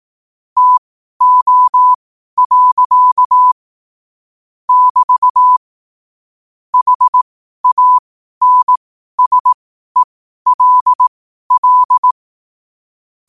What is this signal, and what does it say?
Morse code, heard as a keyed tone, about 18 wpm, spaced as for 12 wpm, 1000 Hz. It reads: TO. <BT> HANSELL